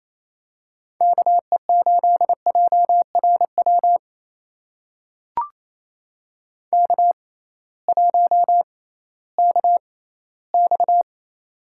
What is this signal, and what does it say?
Morse code, heard as a keyed tone, about 28 wpm, spaced as for 16 wpm, 700 Hz. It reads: KE8JRW K1KX
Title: USA Call Signs - 6x Character Spacing - 28wpm